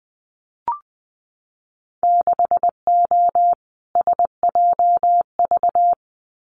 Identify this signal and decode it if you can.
Morse code, heard as a keyed tone, about 20 wpm, 700 Hz. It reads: 6O SJV
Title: Winter Field Day Exchanges Encore 20wpm